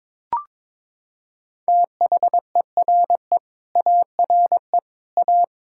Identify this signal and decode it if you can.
Morse code, heard as a keyed tone, about 22 wpm, 700 Hz. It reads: THERE ARE A
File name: practice / 22wpm / Sets of 8 Words - 22wpm